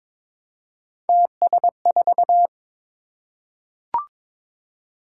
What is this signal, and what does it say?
Morse code, heard as a keyed tone, about 22 wpm, 700 Hz. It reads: TS4
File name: practice / 22wpm / Three Letter-Number Combinations - 22wpm